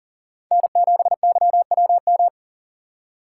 Morse code, heard as a keyed tone, about 40 wpm, 700 Hz. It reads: N7YWM
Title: USA Call Signs - 4x Character Spacing - 40wpm